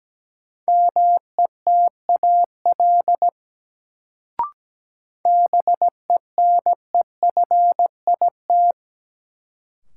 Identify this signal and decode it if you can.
Morse code, heard as a keyed tone, about 17 wpm, 700 Hz. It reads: METAL BENEFIT
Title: Sentences from Top 1000 Words - Review Part 6 - New Word Drill - 17wpm